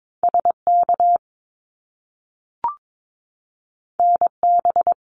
Morse code, heard as a keyed tone, about 22 wpm, 700 Hz. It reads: SK NB
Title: Canadian Provinces and Territories 22wpm